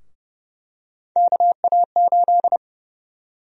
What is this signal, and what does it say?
Morse code, heard as a keyed tone, about 31 wpm, 700 Hz. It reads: KA8
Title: USA Call Sign Prefixes - 30wpm